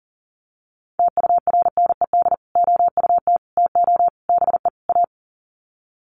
Morse code, heard as a keyed tone, about 40 wpm, 700 Hz. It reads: TURNED OUT TO BE A
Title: Sets of 5 Words 40wpm